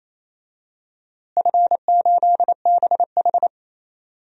Morse code, heard as a keyed tone, about 27 wpm, 700 Hz. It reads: F8BH
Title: DXCC-100 Call Signs - 4 Characters - Repeated 3x - 28wpm